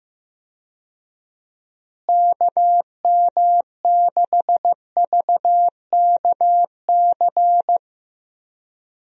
Morse code, heard as a keyed tone, about 15 wpm, 700 Hz. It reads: KM6VKC
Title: USA Call Signs - 6 Characters - Repeated 2x - 15wpm